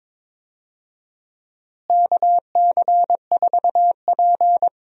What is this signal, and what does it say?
Morse code, heard as a keyed tone, about 22 wpm, 700 Hz. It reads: KC4P
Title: USA Call Signs - 4 Characters - Repeated 2x - 22wpm